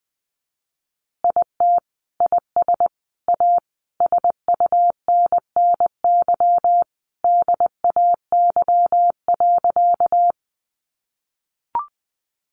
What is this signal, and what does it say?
Morse code, heard as a keyed tone, about 20 wpm, 700 Hz. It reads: IT IS A SUNNY DAY.
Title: Additional Easy Phrases 20wpm